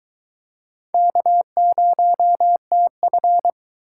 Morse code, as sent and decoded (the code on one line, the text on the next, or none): -.- ----- - ..-.
K0TF